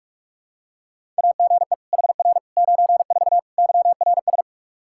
Morse code, heard as a keyed tone, about 46 wpm, 700 Hz. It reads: AGE HR 94 YRS
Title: Practice QSO - Exchange 3 - More Info - 45wpm